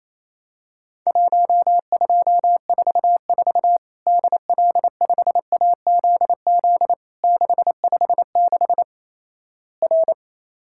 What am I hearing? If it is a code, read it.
Morse code, heard as a keyed tone, about 28 wpm, 700 Hz. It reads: 1244 DL5AZZ 656 R
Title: Worked All Europe DX Contest - QTC Exchange - 28wpm